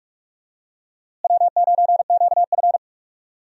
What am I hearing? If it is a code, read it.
Morse code, heard as a keyed tone, about 46 wpm, 700 Hz. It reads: W9QF
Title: USA Call Signs - 4 Characters - Repeated 3x - 45wpm